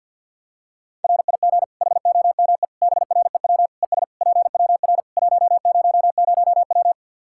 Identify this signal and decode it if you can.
Morse code, heard as a keyed tone, about 52 wpm, 700 Hz. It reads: RIG HOME BREW ES PWR 100W
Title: Practice QSO - Exchange 2 - Info - 50wpm